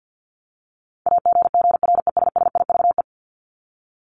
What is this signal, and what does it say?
Morse code, heard as a keyed tone, about 50 wpm, 700 Hz. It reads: AGGRESSIVE
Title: Top 1501-1750 Words 50wpm